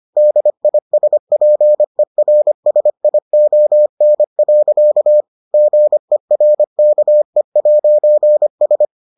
Morse code, heard as a keyed tone, about 25 wpm, 600 Hz. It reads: DISPERSION. GERKE'S